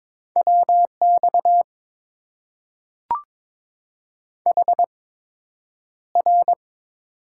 Morse code, heard as a keyed tone, about 22 wpm, spaced as for 9 wpm, 700 Hz. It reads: WX HR
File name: Top 90 QSO Elements - 8x Character Spacing - 22wpm